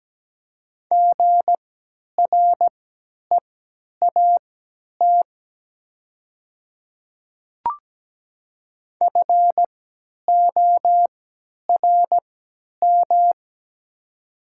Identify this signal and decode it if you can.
Morse code, heard as a keyed tone, about 17 wpm, spaced as for 10 wpm, 700 Hz. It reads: GREAT FORM
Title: 3x Character Spacing - Top 300 Words - 17wpm